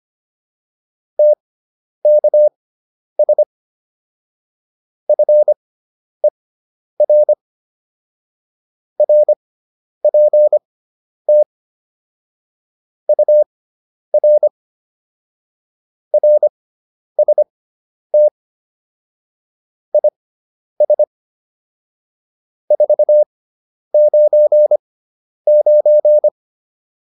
Morse code, heard as a keyed tone, about 25 wpm, spaced as for 10 wpm, 600 Hz. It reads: TKS FER RPT UR RST IS 499